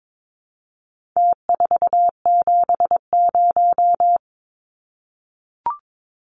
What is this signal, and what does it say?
Morse code, heard as a keyed tone, about 22 wpm, 700 Hz. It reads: T470
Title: Numbers - Contest Style - 4-Digit 22wpm